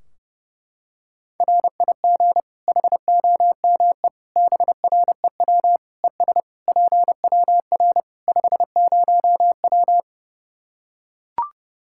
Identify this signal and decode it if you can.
Morse code, heard as a keyed tone, about 32 wpm, 700 Hz. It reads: RIG HOME BREW ES PWR 50W
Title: K1USN SST QSO Practice 30wpm